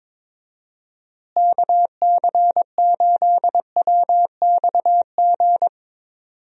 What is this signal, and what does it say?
Morse code, heard as a keyed tone, about 22 wpm, 700 Hz. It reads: KC8WXG
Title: USA Call Signs - 6 Characters - Repeated 2x - 22wpm